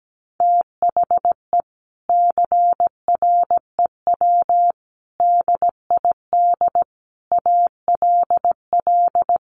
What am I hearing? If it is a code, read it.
Morse code, heard as a keyed tone, about 17 wpm, 700 Hz. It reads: THE CREW DID ALL
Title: Sentences from Top 1500 Words 17wpm